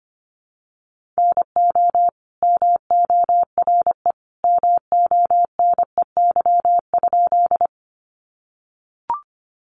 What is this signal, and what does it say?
Morse code, heard as a keyed tone, about 25 wpm, 700 Hz. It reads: NO MORE MONEY?
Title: Sentences from Top 400 Words 25wpm